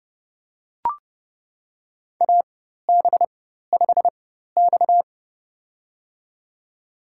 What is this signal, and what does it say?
Morse code, heard as a keyed tone, about 32 wpm, spaced as for 14 wpm, 700 Hz. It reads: AB5X
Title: USA Call Signs - 4 Characters - 4x Character Spacing - 30wpm